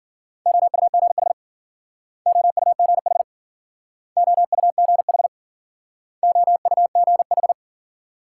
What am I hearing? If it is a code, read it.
Morse code, heard as a keyed tone, about 52 wpm, 700 Hz. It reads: OUGH OUGH OUGH OUGH